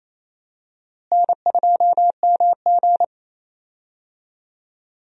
Morse code, heard as a keyed tone, about 28 wpm, 700 Hz. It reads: N2MG
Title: USA Call Signs - 4 Characters - Repeated 3x - 28wpm